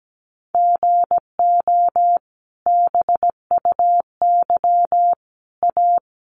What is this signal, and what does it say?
Morse code, heard as a keyed tone, about 17 wpm, 700 Hz. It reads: GO BUY A
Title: Sentences from Top 1200 Words 17wpm